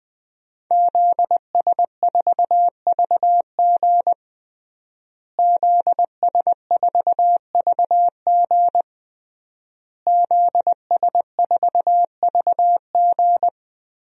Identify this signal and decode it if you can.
Morse code, heard as a keyed tone, about 20 wpm, 700 Hz. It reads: ZS4VG ZS4VG ZS4VG
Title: DXCC-100 Call Signs - 5 Characters - Repeated 3x - 20wpm